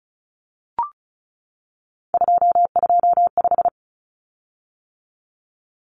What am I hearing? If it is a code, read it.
Morse code, heard as a keyed tone, about 34 wpm, 700 Hz. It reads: 225
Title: Common Cut Numbers - 3-Digit 35wpm